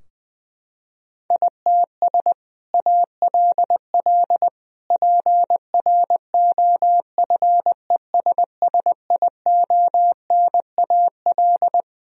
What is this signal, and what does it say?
Morse code, heard as a keyed tone, about 20 wpm, 700 Hz. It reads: ITS ALL PROFESSIONAL
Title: Sentences from Top 1000 Words - Review Part 4 - 20wpm